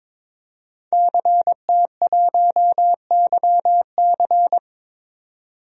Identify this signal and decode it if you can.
Morse code, heard as a keyed tone, about 22 wpm, 700 Hz. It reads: CT1YC
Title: DXCC-100 Call Signs - 5 Characters - 22wpm